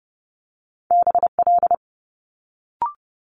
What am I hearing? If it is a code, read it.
Morse code, heard as a keyed tone, about 30 wpm, 700 Hz. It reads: BL
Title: Multi-letter Phonemes Quick Fire 30wpm